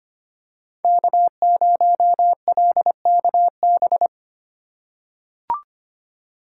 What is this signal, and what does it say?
Morse code, heard as a keyed tone, about 25 wpm, 700 Hz. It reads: K0LKB
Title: USA Call Signs - 5 Characters - 2x Character Spacing - 25wpm